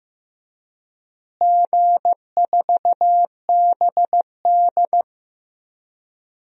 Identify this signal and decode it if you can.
Morse code, heard as a keyed tone, about 15 wpm, 700 Hz. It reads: G4BD
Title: DXCC-100 Call Signs - 4 Characters - Repeated 3x - 15wpm